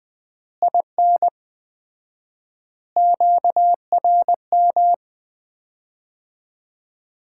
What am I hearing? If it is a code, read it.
Morse code, heard as a keyed tone, about 21 wpm, 700 Hz. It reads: IN QRM
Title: Common QSO Phrases - 4x Word-Spacing - 20wpm